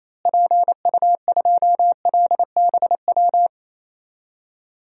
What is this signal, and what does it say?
Morse code, heard as a keyed tone, about 28 wpm, 700 Hz. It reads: PU2LBW
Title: DXCC-100 Call Signs - 6 Characters - Repeated 3x - 28wpm